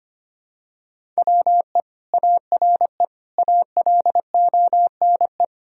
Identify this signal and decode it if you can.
Morse code, heard as a keyed tone, about 25 wpm, 700 Hz. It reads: WE ARE ALONE
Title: Sentences from Top 1000 Words - Review Part 1 - 25wpm